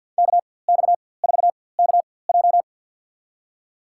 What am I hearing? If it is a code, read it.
Morse code, heard as a keyed tone, about 53 wpm, spaced as for 24 wpm, 700 Hz. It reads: KX4KJ
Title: USA Call Signs - 5 Characters - 4x Character Spacing - 50wpm